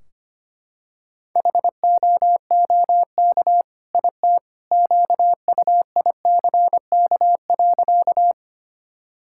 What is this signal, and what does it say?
Morse code, heard as a keyed tone, about 25 wpm, 700 Hz. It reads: HOOK IT QUICK.
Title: Sentences from Top 1600 Words 25wpm